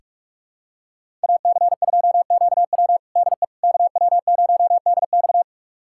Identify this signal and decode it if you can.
Morse code, heard as a keyed tone, about 41 wpm, 700 Hz. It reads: AC2QW DE KW0DX
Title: USA Callsign DE Callsign - 5 Characters - 45wpm